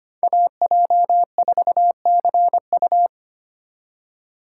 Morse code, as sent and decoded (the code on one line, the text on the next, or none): .- .--- ....- -.-. ..-
AJ4CU